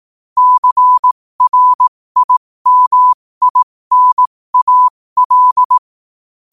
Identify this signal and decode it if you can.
Morse code, heard as a keyed tone, about 18 wpm, spaced as for 15 wpm, 1000 Hz. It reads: CRIMINAL
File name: qotdfiles / 15wpm